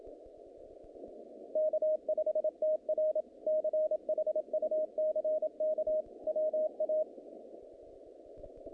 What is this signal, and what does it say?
Morse code, heard as a keyed tone, about 27 wpm, 620 Hz. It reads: K5TR CHUCK WA
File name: contests / naqp-cw-k5tr-op-k5pi- -2200